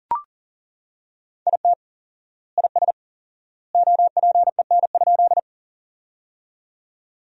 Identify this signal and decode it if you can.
Morse code, heard as a keyed tone, about 40 wpm, 700 Hz. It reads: IT IS OPEN?
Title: 4x Word-Spacing - Sentences from Top 300 Words - 40wpm